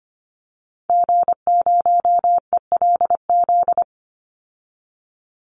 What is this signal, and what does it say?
Morse code, heard as a keyed tone, about 25 wpm, 700 Hz. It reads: G0ELZ
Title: Call Signs - 25wpm